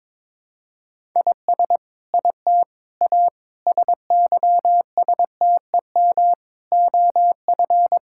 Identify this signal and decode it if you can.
Morse code, heard as a keyed tone, about 22 wpm, 700 Hz. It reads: IS IT A SYSTEM OF
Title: Sentences from Top 1000 Words - Review Part 6 - 22wpm